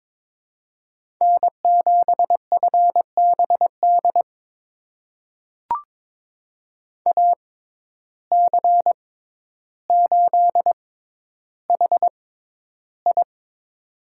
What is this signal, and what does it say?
Morse code, heard as a keyed tone, about 22 wpm, spaced as for 11 wpm, 700 Hz. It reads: N7FBD AC8HI
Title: USA Call Signs - 5 Characters - 6x Character Spacing - 22wpm